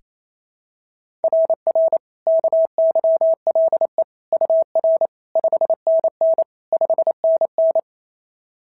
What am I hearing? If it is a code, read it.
Morse code, heard as a keyed tone, about 25 wpm, 650 Hz. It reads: RR KYLE UR 5NN 5NN
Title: SOTA QSO Practice 28wpm